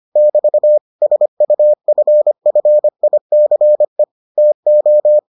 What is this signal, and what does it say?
Morse code, heard as a keyed tone, about 25 wpm, 600 Hz. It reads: <BT> SUFFICE TO